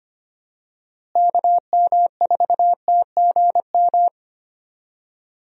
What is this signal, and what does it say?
Morse code, heard as a keyed tone, about 25 wpm, 700 Hz. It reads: KM4TGM
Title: USA Call Signs - 6 Characters - Repeated 2x - 25wpm